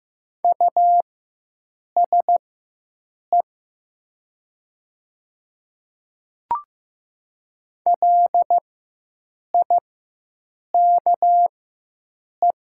Morse code, heard as a keyed tone, about 15 wpm, spaced as for 7 wpm, 700 Hz. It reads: USE LIKE
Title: 4x Character Spacing - Top 100 Words - 15wpm